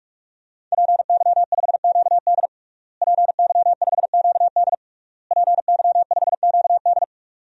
Morse code, heard as a keyed tone, about 46 wpm, 700 Hz. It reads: PY5QD PY5QD PY5QD
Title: DXCC-100 Call Signs - 5 Characters - Repeated 3x - 45wpm